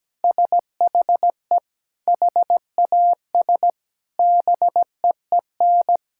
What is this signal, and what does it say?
Morse code, heard as a keyed tone, about 17 wpm, 700 Hz. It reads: SHE HAS BEEN
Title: Sentences from Top 1000 Words - Review Part 6 - 17wpm